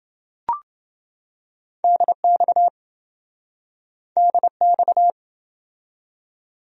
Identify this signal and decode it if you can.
Morse code, heard as a keyed tone, about 27 wpm, 700 Hz. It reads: DX DX